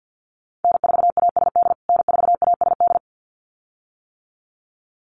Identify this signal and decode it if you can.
Morse code, heard as a keyed tone, about 50 wpm, 700 Hz. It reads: N4ASD N4ASD
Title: USA Call Signs Repeated 2x 50wpm